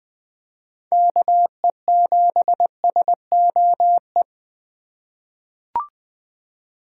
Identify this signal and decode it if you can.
Morse code, heard as a keyed tone, about 20 wpm, 700 Hz. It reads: KE7SOE
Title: USA Call Signs - 6x Character Spacing - 20wpm